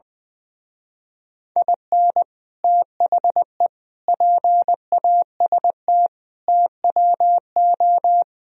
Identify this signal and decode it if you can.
Morse code, heard as a keyed tone, about 21 wpm, 700 Hz. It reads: IN THE PAST TWO
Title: Speed-Racing - Sets of 5 Words 20wpm